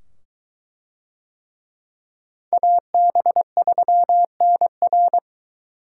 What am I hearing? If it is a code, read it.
Morse code, heard as a keyed tone, about 22 wpm, 700 Hz. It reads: AB3NR